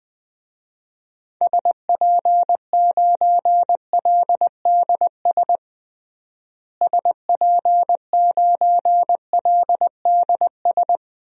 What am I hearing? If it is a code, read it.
Morse code, heard as a keyed tone, about 20 wpm, 700 Hz. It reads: SP9LDS SP9LDS
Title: DXCC-100 Call Signs - 6 Characters - Repeated 2x - 20wpm